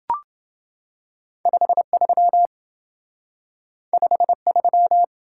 Morse code, heard as a keyed tone, about 27 wpm, 700 Hz. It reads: 53 53